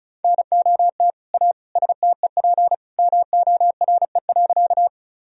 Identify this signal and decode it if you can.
Morse code, heard as a keyed tone, about 36 wpm, 700 Hz. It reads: NOT A STEP MORE.
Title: Sentences from Top 1000 Words - Review Part 2 - 35wpm